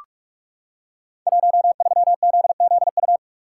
Morse code, heard as a keyed tone, about 44 wpm, 700 Hz. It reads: 13ZZU
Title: Code Groups - 5 Letter:Numbers - 45wpm